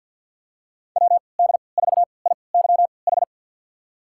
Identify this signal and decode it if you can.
Morse code, heard as a keyed tone, about 46 wpm, spaced as for 27 wpm, 700 Hz. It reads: WD4IYH
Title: USA Call Signs - 3x Character Spacing - 50wpm